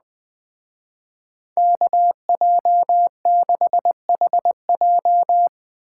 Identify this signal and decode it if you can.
Morse code, heard as a keyed tone, about 19 wpm, 700 Hz. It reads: KJ6HJ
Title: Speed Racing - USA 2x2 Call Signs 17wpm